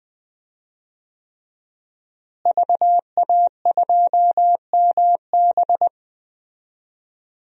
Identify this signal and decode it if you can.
Morse code, heard as a keyed tone, about 20 wpm, 700 Hz. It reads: VA2MB
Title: DXCC-100 Call Signs - 5 Characters - Repeated 3x - 20wpm